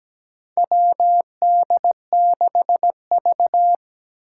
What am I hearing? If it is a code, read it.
Morse code, heard as a keyed tone, about 17 wpm, 700 Hz. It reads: WD6V
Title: Four Challenging Letter-Number Combinations 17wpm